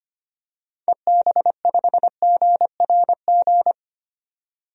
Morse code, heard as a keyed tone, about 25 wpm, 700 Hz. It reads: EB5GRG
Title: DXCC-100 Call Signs - 6 Characters - 25wpm